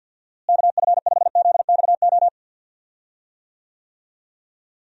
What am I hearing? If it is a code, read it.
Morse code, heard as a keyed tone, about 50 wpm, 700 Hz. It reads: KF5ZXO
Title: USA Call Signs - 6 Characters - Repeated 3x - 50wpm